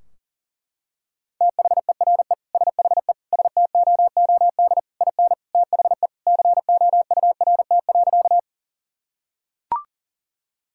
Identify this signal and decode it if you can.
Morse code, heard as a keyed tone, about 37 wpm, 700 Hz. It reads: THERE SHE STOOD IN THE COURT.
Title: Sentences from Top 1600 Words 40wpm